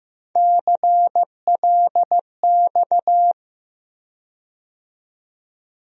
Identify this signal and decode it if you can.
Morse code, heard as a keyed tone, about 15 wpm, 700 Hz. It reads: CLX
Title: USA Call Sign Postfixes - 3 Letters 15wpm